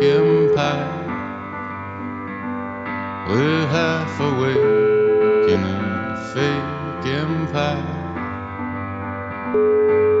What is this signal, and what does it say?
Ringback tone (Europe, 425 Hz), with noise.